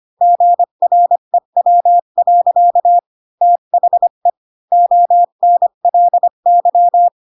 Morse code, heard as a keyed tone, about 25 wpm, 700 Hz. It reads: GREW. THE ONLY
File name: Pinocchio / Pinocchio ch5 20 25 ik1hge